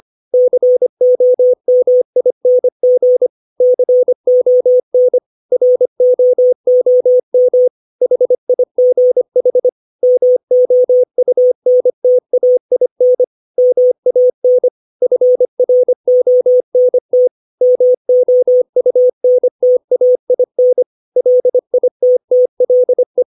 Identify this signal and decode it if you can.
Morse code, heard as a keyed tone, about 25 wpm, 500 Hz. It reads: COMING CON ROOM HIGH MOUNTAIN MAN FRONT MOUNTAIN LITTLE
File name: 632 words / 632 Words 25WPM